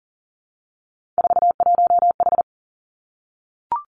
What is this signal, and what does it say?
Morse code, heard as a keyed tone, about 37 wpm, 700 Hz. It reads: 41H
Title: Three Challenging Letter-Number Combinations 40wpm